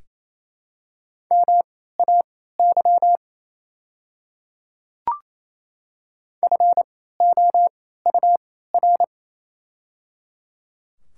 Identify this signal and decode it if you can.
Morse code, heard as a keyed tone, about 28 wpm, spaced as for 16 wpm, 700 Hz. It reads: MAY FOUR
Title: 3x Character Spacing - Top 300 Words - 28wpm